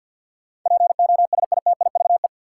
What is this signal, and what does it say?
Morse code, heard as a keyed tone, about 50 wpm, 700 Hz. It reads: POSITIVE
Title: Sentences from Top 1000 Words - Review Part 5 - New Word Drill - 50wpm